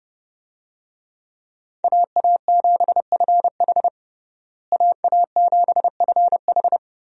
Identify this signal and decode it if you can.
Morse code, heard as a keyed tone, about 32 wpm, 700 Hz. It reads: AA7FH AA7FH
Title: USA Call Signs - 5 Characters - Repeated 3x - 30wpm